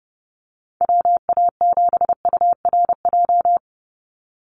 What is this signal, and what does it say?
Morse code, heard as a keyed tone, about 30 wpm, 700 Hz. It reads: WA7URJ
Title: USA Call Signs 30wpm